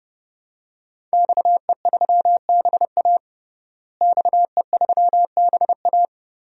Morse code, heard as a keyed tone, about 30 wpm, 700 Hz. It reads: XE3BA XE3BA
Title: DXCC-100 Call Signs - 5 Characters - Repeated 3x - 30wpm